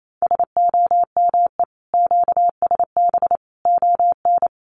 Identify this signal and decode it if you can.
Morse code, heard as a keyed tone, about 28 wpm, 700 Hz. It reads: SOME QSB ON
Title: Common QSO Phrases 28wpm